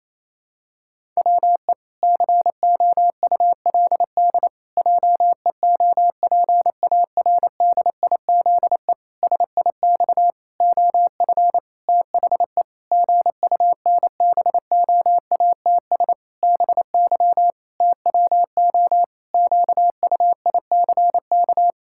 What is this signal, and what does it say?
Morse code, heard as a keyed tone, about 28 wpm, 700 Hz. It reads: WE COULD JEOPARDIZE SIX OF THE GUNBOATS BY TWO QUICK
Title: Pangram Challenge v2 - 28wpm